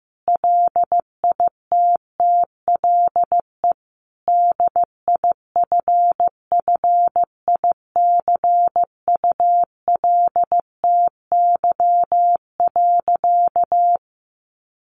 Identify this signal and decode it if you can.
Morse code, heard as a keyed tone, about 15 wpm, 700 Hz. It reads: LITTLE DIFFICULTY.
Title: Sentences from Top 1500 Words 15wpm